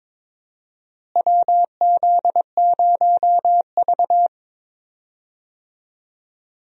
Morse code, heard as a keyed tone, about 22 wpm, 700 Hz. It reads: WZ0V
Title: USA Call Signs - 4 Characters - Repeated 3x - 22wpm